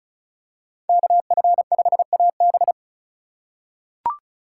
Morse code, heard as a keyed tone, about 34 wpm, 700 Hz. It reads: KF5AB
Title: USA Call Signs - 2x Character Spacing - 35wpm